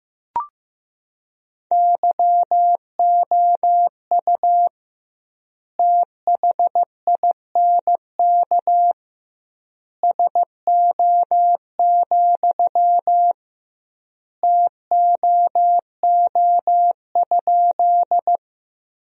Morse code, heard as a keyed tone, about 15 wpm, 700 Hz. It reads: YOU THINK SO, TOO?
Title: 2x Word-Spacing - Sentences from Top 200 Words - 15wpm